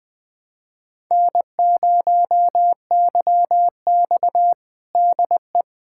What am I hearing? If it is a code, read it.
Morse code, heard as a keyed tone, about 21 wpm, 700 Hz. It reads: N0YX DE
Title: USA Callsign DE Callsign - 4 Characters - 20wpm